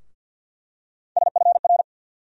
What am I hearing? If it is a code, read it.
Morse code, heard as a keyed tone, about 50 wpm, 700 Hz. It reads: SFR
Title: USA Call Sign Postfixes - 3 Letters 50wpm